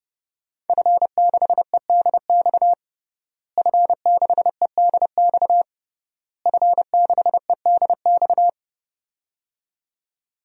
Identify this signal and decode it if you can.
Morse code, heard as a keyed tone, about 30 wpm, 700 Hz. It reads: F6EDX F6EDX F6EDX
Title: DXCC-100 Call Signs - 5 Characters - Repeated 3x - 30wpm